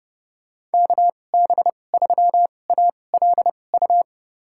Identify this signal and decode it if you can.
Morse code, heard as a keyed tone, about 30 wpm, spaced as for 22 wpm, 700 Hz. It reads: KB3ALU
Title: USA Call Signs - 6 Characters - 2x Character Spacing - 30wpm